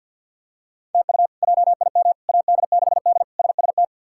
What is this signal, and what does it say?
Morse code, heard as a keyed tone, about 52 wpm, 700 Hz. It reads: TU JIM AD6D SST
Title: K1USN SST QSO Practice - 50wpm